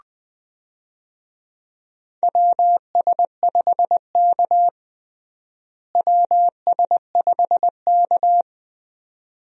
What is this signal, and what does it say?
Morse code, heard as a keyed tone, about 20 wpm, 700 Hz. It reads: WS5K WS5K
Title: USA Call Signs - 4 Characters - Repeated 2x - 20wpm